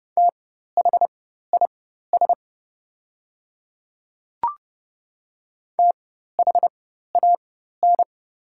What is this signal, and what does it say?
Morse code, heard as a keyed tone, about 30 wpm, spaced as for 14 wpm, 690 Hz. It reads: THIS THAN
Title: 4x Character Spacing - Top 100 Words - 30wpm